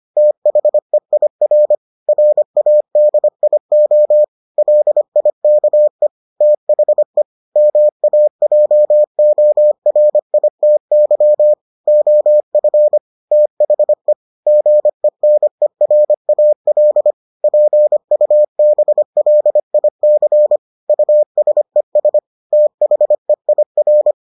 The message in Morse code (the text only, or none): THEIR RADIO LIKE THE MAJORITY OF THE GENERAL PUBLIC USES THEIR